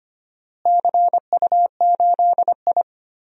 Morse code, heard as a keyed tone, about 25 wpm, 700 Hz. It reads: CU8I
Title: DXCC-100 Call Signs - 4 Characters - Repeated 3x - 25wpm